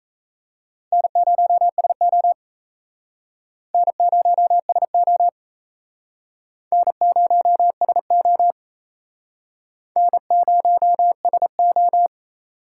Morse code, heard as a keyed tone, about 35 wpm, 700 Hz. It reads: N0SO N0SO N0SO N0SO
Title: Speed Racing - USA 1x2 Call Signs 28wpm